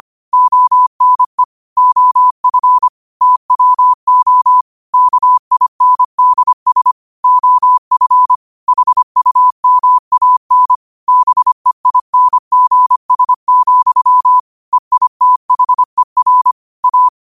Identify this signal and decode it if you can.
Morse code, heard as a keyed tone, about 25 wpm, 1000 Hz. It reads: ONE OF TWO KINDS OF HUMAN BEINGS, EITHER A
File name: qotdfiles / 25wpm